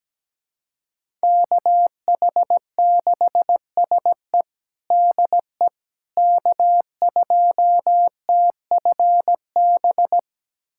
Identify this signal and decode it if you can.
Morse code, heard as a keyed tone, about 17 wpm, 700 Hz. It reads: KH6SE DE K2TFB
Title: USA Callsign DE Callsign - 5 Characters - 17wpm